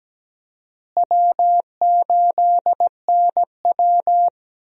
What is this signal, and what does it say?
Morse code, heard as a keyed tone, about 17 wpm, 700 Hz. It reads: W8NW
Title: USA Call Signs - 4 Characters - Repeated 3x - 17wpm